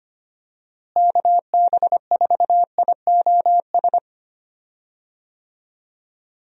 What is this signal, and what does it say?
Morse code, heard as a keyed tone, about 25 wpm, 700 Hz. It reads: KB4IOS
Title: USA Call Signs - 6 Characters - Repeated 2x - 25wpm